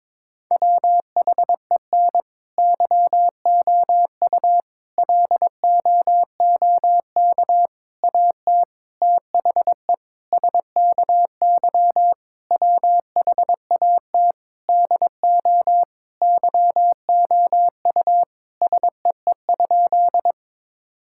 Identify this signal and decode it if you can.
Morse code, heard as a keyed tone, about 22 wpm, 700 Hz. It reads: WHEN YOU LOOK AT THE SKY WHAT DO YOU SEE?
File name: practice / 22wpm / Sentences with 3-Word Phrases - 22wpm